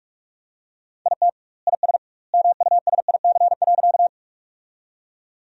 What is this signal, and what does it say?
Morse code, heard as a keyed tone, about 43 wpm, 700 Hz. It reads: IT IS MUSIC.
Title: 2x Word-Spacing - Sentences from Top 300 Words - 45wpm